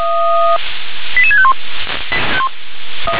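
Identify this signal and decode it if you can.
Five-tone selective call (ZVEI), heard as quick runs of five tones, tones of about 70 ms. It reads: 88651
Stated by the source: ZVEI five-tone call received by SDR.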